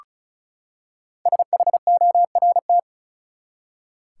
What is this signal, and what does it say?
Morse code, heard as a keyed tone, about 36 wpm, 700 Hz. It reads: SHORT
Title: Five Letter Words - 35wpm